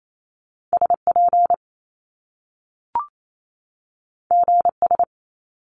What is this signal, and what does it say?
Morse code, heard as a keyed tone, about 28 wpm, 700 Hz. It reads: SP GS
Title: Top Two Letter Combinations in Words 28wpm